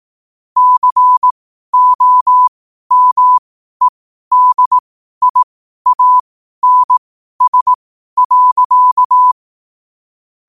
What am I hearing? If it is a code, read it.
Morse code, heard as a keyed tone, about 18 wpm, spaced as for 12 wpm, 1000 Hz. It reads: COMEDIANS.
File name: qotdfiles / 12wpm